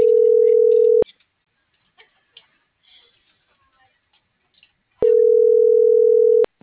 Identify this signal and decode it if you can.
Ringback tone (North America), 440 + 480 Hz.